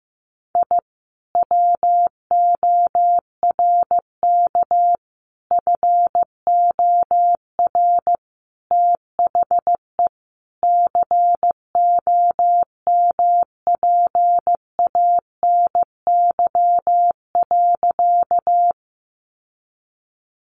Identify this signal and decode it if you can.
Morse code, heard as a keyed tone, about 14 wpm, 700 Hz. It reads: I WORK FOR THE COMPANY.
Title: Sentences from Top 600 Words 15wpm